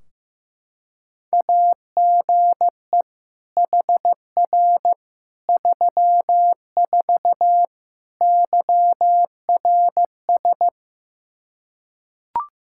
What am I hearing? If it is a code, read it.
Morse code, heard as a keyed tone, about 15 wpm, 700 Hz. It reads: AGE HR 34 YRS
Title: Practice QSO - Exchange 3 - More Info - 15wpm